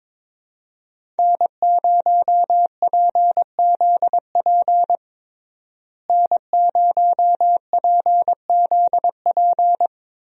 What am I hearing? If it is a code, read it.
Morse code, heard as a keyed tone, about 22 wpm, 700 Hz. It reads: N0PZP N0PZP
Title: USA Call Signs - 5 Characters - Repeated 3x - 22wpm